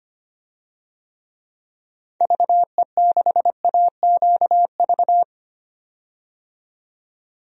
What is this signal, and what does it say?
Morse code, heard as a keyed tone, about 25 wpm, 700 Hz. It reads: VE6AQV
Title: DXCC-100 Call Signs - 6 Characters - 25wpm